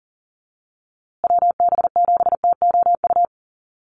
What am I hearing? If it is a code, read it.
Morse code, heard as a keyed tone, about 40 wpm, 700 Hz. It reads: WB7TOU
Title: USA Call Signs Repeated 2x 40wpm